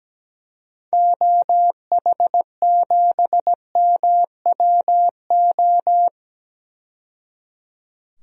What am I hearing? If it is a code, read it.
Morse code, heard as a keyed tone, about 17 wpm, 700 Hz. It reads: OH7MWO
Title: DXCC-100 Call Signs - 6 Characters - 17wpm